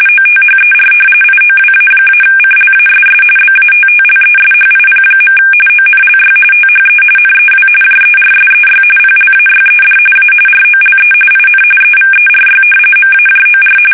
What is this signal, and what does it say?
RTTY, about 45.45 baud. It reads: OTHER IOWA-CLASS BATTLESHIP. ONLY THE NEW JERSEY PLAYED A ROLE IN EVERY MAJOR
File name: fsk850 / battleshipnj